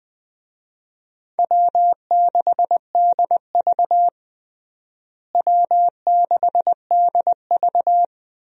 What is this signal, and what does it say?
Morse code, heard as a keyed tone, about 21 wpm, 700 Hz. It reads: W6DV W6DV
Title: USA Call Signs - 4 Characters - Repeated 2x - 20wpm